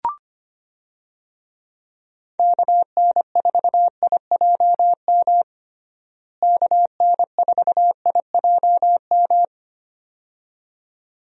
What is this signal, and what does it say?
Morse code, heard as a keyed tone, about 25 wpm, 700 Hz. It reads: KN4IJM KN4IJM
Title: USA Call Signs - 6 Characters - Repeated 2x - 25wpm